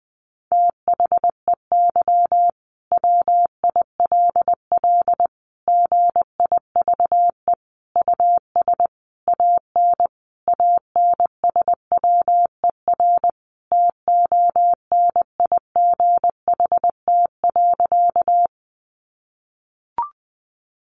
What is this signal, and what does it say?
Morse code, heard as a keyed tone, about 20 wpm, 700 Hz. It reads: THEY WILL GIVE US AN ANSWER TONIGHT.
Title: Sentences from Top 600 Words 20wpm